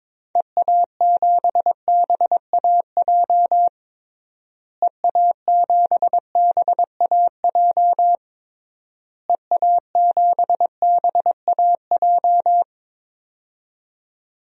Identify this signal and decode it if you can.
Morse code, heard as a keyed tone, about 22 wpm, 700 Hz. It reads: EA7BAJ EA7BAJ EA7BAJ
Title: DXCC-100 Call Signs - 6 Characters - Repeated 3x - 22wpm